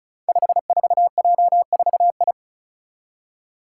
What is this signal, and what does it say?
Morse code, heard as a keyed tone, about 36 wpm, 700 Hz. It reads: 54J4I
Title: Five Challenging Letter-Number Combinations 35wpm